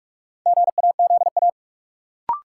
Morse code, heard as a keyed tone, about 46 wpm, 700 Hz. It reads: GAZA
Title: Sets of 5 Words - New Word Drill - 45wpm